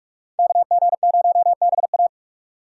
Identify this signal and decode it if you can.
Morse code, heard as a keyed tone, about 47 wpm, 700 Hz. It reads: KG0BA
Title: USA Call Signs - 3x Character Spacing - 45wpm